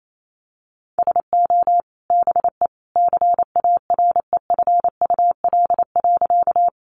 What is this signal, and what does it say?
Morse code, heard as a keyed tone, about 28 wpm, 700 Hz. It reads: SO BE CAREFUL.
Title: A Sound Of Thunder 28wpm